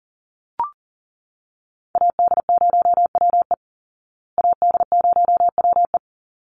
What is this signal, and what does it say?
Morse code, heard as a keyed tone, about 40 wpm, 700 Hz. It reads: AD0WE AD0WE
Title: Call Signs Repeated 3x - 40wpm